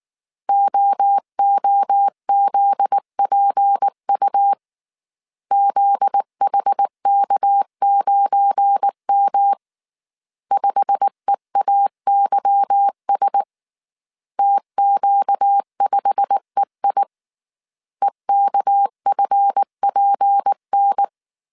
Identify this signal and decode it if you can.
Morse code, heard as a keyed tone, about 19 wpm, 800 Hz. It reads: OOZPU ZHK9M 5EAYS TQ5EI EKFPN
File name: CW / DET / 1 9 1500C S A 18wpm 2459